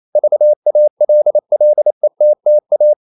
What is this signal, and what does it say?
Morse code, heard as a keyed tone, about 28 wpm, 600 Hz. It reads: VALLETTA
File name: CW / Practice / NationalCapitals28